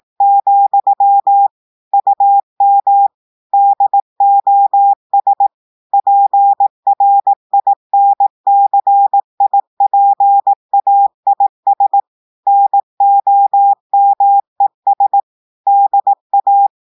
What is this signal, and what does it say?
Morse code, heard as a keyed tone, about 18 wpm, 800 Hz. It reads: , UM DOS PRINCIPAIS NOMES DA
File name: audios / portugais / portugais texte en clair 1 vitesse 18 mots minutes